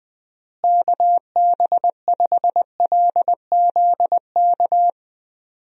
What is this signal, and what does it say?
Morse code, heard as a keyed tone, about 20 wpm, 700 Hz. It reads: KB5LZK
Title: USA Call Signs - 6 Characters - Repeated 2x - 20wpm